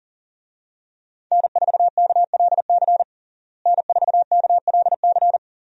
Morse code, heard as a keyed tone, about 37 wpm, 700 Hz. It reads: N4KLC N4KLC
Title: USA Call Signs - 5 Characters - Repeated 2x - 40wpm